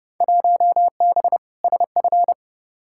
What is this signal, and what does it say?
Morse code, heard as a keyed tone, about 31 wpm, 700 Hz. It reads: 1B SF
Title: Field Day Exchanges Encore - 30wpm